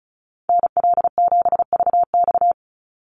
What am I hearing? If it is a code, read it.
Morse code, heard as a keyed tone, about 34 wpm, 700 Hz. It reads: NL7VX
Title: Call Signs - 35wpm